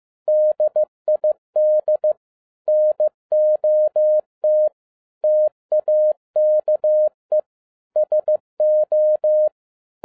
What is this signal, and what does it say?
Morse code, heard as a keyed tone, about 15 wpm, 610 Hz. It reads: DID NOT TAKE SO
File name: 2011-07-mp3 / n42 EXCITING QRP CW OPERATION 15wpm